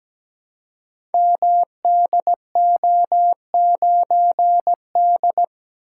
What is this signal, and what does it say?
Morse code, heard as a keyed tone, about 17 wpm, 700 Hz. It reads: MDO9D
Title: Code Groups - 5 Letter:Numbers - 17wpm